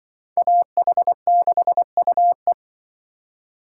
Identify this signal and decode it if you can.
Morse code, heard as a keyed tone, about 24 wpm, 700 Hz. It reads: AH6UE
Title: Speed Racing - USA 2x2 Call Signs 20wpm